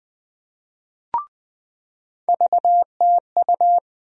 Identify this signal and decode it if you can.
Morse code, heard as a keyed tone, about 20 wpm, 700 Hz. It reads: VTU
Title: USA Call Sign Postfixes - 3 Letters 20wpm